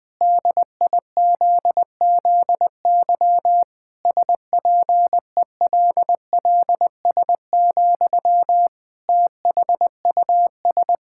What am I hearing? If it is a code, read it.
Morse code, heard as a keyed tone, about 20 wpm, 700 Hz. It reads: DIZZY SPELLS, THUS
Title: Pangram Challenge v2 - 20wpm